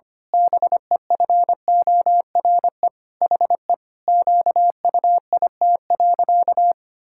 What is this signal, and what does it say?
Morse code, heard as a keyed tone, about 25 wpm, 700 Hz. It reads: BEFORE HE QUIT.
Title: Pangram Challenge v2 - 25wpm